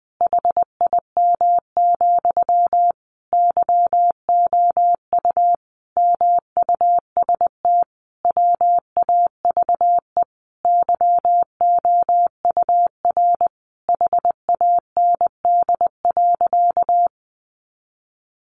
Morse code, heard as a keyed tone, about 20 wpm, 700 Hz. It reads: HIM, YOU MUST WAVE YOUR HAND.
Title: Sentences from Top 1400 Words 20wpm